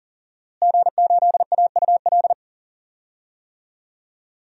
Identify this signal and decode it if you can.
Morse code, heard as a keyed tone, about 37 wpm, 700 Hz. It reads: G8AUL
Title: DXCC-100 Call Signs - 5 Characters - Repeated 2x - 40wpm